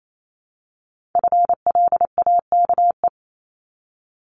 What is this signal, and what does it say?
Morse code, heard as a keyed tone, about 28 wpm, 700 Hz. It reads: FLAKE
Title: Multi-letter Phonemes Repeated 3x 28wpm